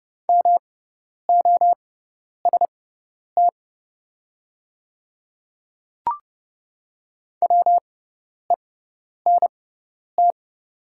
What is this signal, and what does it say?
Morse code, heard as a keyed tone, about 30 wpm, spaced as for 10 wpm, 690 Hz. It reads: MOST WENT
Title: 6x Character Spacing - Top 200 Words - 30wpm